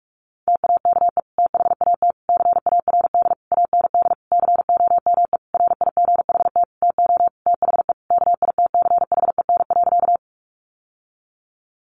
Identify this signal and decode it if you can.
Morse code, heard as a keyed tone, about 44 wpm, 700 Hz. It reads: TAKE THAT CARD AND COME RIGHT TO THE KITCHEN.
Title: Sentences from Top 1400 Words 45wpm